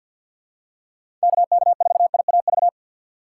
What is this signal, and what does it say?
Morse code, heard as a keyed tone, about 51 wpm, 700 Hz. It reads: KK4IAV
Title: USA Call Signs - 6 Characters - 4x Character Spacing - 50wpm